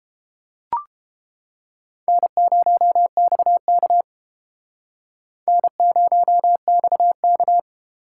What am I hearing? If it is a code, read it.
Morse code, heard as a keyed tone, about 29 wpm, 700 Hz. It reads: N0XK N0XK